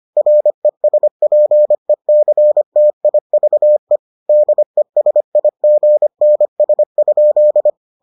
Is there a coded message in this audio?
Yes: Morse code from 0.2 to 7.7 s, heard as a keyed tone, about 25 wpm, 600 Hz. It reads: RESPECTIVE DESIGNS?